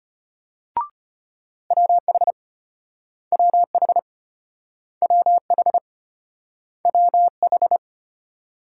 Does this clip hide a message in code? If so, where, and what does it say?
Yes: Morse code from 1.7 to 7.8 s, heard as a keyed tone, about 34 wpm, 700 Hz. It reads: WH WH WH WH